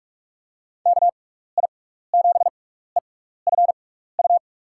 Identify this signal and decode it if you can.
Morse code, heard as a keyed tone, about 46 wpm, spaced as for 16 wpm, 700 Hz. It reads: KI7EFU
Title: USA Call Signs - 6x Character Spacing - 45wpm